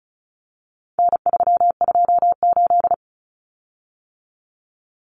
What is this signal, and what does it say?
Morse code, heard as a keyed tone, about 34 wpm, 700 Hz. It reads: N328
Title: Common Cut Numbers - 4-Digit 35wpm